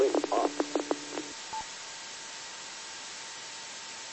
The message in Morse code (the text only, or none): none